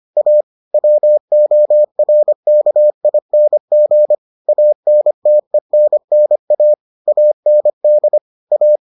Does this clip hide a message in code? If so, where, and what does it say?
Yes: Morse code from 0.2 to 8.8 s, heard as a keyed tone, about 25 wpm, 600 Hz. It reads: A WORKING ANTENNA AND A